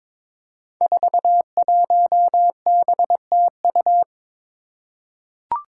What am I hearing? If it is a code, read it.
Morse code, heard as a keyed tone, about 22 wpm, 700 Hz. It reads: 41BTU
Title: Five Challenging Letter-Number Combinations 22wpm